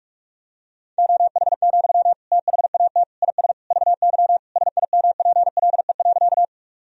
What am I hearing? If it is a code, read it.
Morse code, heard as a keyed tone, about 46 wpm, 690 Hz. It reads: OH, THAT IS VY SIMPLE.
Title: Sentences from Top 500 Words and Abbreviations 45wpm